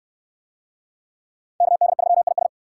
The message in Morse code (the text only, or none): KD3IS